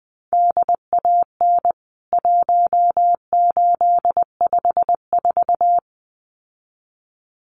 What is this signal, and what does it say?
Morse code, heard as a keyed tone, about 20 wpm, 700 Hz. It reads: DAN 1854
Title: CWT Exchange 20wpm